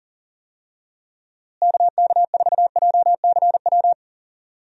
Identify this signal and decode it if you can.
Morse code, heard as a keyed tone, about 39 wpm, 700 Hz. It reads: KK4JCW
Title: USA Call Signs - 6 Characters - Repeated 2x - 40wpm